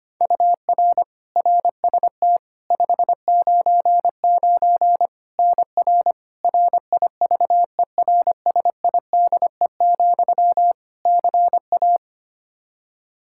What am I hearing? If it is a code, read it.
Morse code, heard as a keyed tone, about 25 wpm, 700 Hz. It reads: UR RST 599 NR RIVERSIDE, CA
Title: Practice QSO - Exchange 1 - Beginning - 25wpm